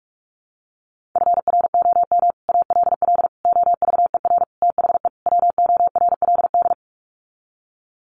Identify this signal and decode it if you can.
Morse code, heard as a keyed tone, about 44 wpm, 700 Hz. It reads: FROM ALL OVER THE WORLD
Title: Sets of 5 Words 45wpm